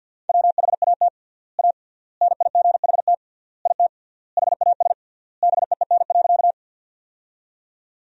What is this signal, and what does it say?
Morse code, heard as a keyed tone, about 48 wpm, 700 Hz. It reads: WHAT A NIGHT IT HAS BEEN.
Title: 3x Word-Spacing - Sentences from Top 300 Words - 50wpm